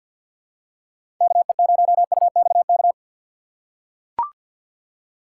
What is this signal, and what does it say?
Morse code, heard as a keyed tone, about 45 wpm, 700 Hz. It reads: KE0UXK
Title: USA Call Signs - 6 Characters - 3x Character Spacing - 50wpm